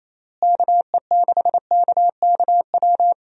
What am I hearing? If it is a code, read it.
Morse code, heard as a keyed tone, about 28 wpm, 700 Hz. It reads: KE6KKW
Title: USA Call Signs - 6 Characters - Repeated 3x - 28wpm